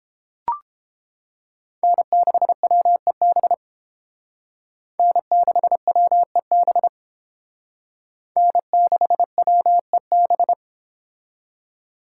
Morse code, heard as a keyed tone, about 28 wpm, 700 Hz. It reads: N6WEB N6WEB N6WEB